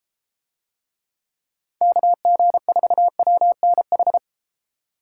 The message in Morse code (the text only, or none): KG4WNH